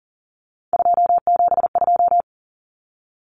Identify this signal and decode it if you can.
Morse code, heard as a keyed tone, about 40 wpm, 700 Hz. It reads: 272
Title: Numbers - 3-Digit 40wpm